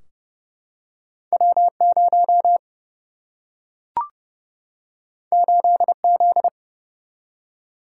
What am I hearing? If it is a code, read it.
Morse code, heard as a keyed tone, about 30 wpm, 700 Hz. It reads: W 0 8 Z
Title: Single-Letter Single-Number 30wpm